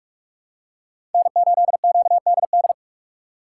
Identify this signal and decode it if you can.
Morse code, heard as a keyed tone, about 42 wpm, 700 Hz. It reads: N8QDD
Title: USA Call Signs - 5 Characters - 4x Character Spacing - 45wpm